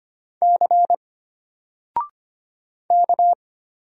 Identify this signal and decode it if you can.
Morse code, heard as a keyed tone, about 25 wpm, 700 Hz. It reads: C K
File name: Single-Letter-Number-25wpm